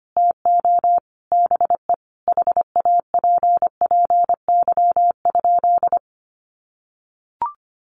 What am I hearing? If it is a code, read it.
Morse code, heard as a keyed tone, about 25 wpm, 700 Hz. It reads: TO BE HAPPY?
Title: Sentences from Top 1100 Words 25wpm